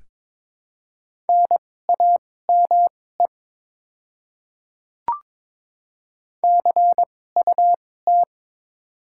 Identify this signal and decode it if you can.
Morse code, heard as a keyed tone, about 22 wpm, spaced as for 16 wpm, 700 Hz. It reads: NAME CUT
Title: 2x Character Spacing - Top 300 Words - 22wpm